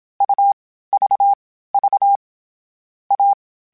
Morse code, heard as a keyed tone, about 27 wpm, 800 Hz. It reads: U V V A
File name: Difficult Combos / Difficult Combos-mp3s / AUV AUV4 @25